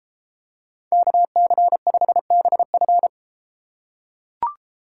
Morse code, heard as a keyed tone, about 32 wpm, 700 Hz. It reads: KC5BF